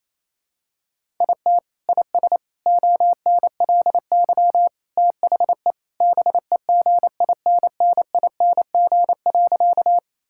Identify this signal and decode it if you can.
Morse code, heard as a keyed tone, about 28 wpm, 700 Hz. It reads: IT IS ONLY THE BEGINNING.
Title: Sentences from Top 1000 Words - Review Part 5 - 28wpm